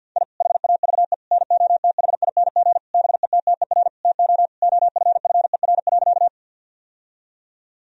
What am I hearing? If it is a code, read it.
Morse code, heard as a keyed tone, about 50 wpm, 700 Hz. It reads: I HAVE NOTHING BETTER TO OFFER.
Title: Sentences from Top 1000 Words - Review Part 4 - 50wpm